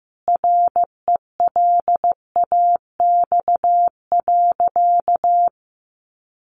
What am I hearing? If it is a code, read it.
Morse code, heard as a keyed tone, about 15 wpm, 700 Hz. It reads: RELAX.
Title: A Sound Of Thunder 15wpm